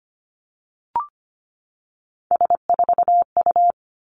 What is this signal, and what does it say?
Morse code, heard as a keyed tone, about 25 wpm, 700 Hz. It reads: S4U
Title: Three Challenging Letter-Number Combinations 25wpm